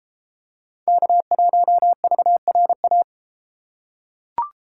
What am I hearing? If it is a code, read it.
Morse code, heard as a keyed tone, about 31 wpm, 700 Hz. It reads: K1VRA